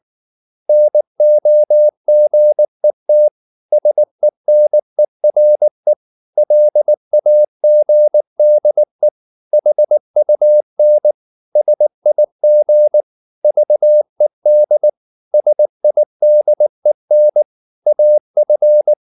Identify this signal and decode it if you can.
Morse code, heard as a keyed tone, about 19 wpm, 600 Hz. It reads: NOGET SENERE LAGDE HUN SIG VED SIDEN AF
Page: Ferie 19 wpm